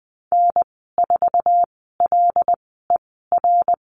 Morse code, heard as a keyed tone, about 20 wpm, spaced as for 15 wpm, 700 Hz. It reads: N4LER
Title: 082 Lesson 14g - Number 2 - Call Signs 20wpm